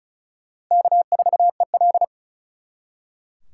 Morse code, heard as a keyed tone, about 44 wpm, 700 Hz. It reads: K4EL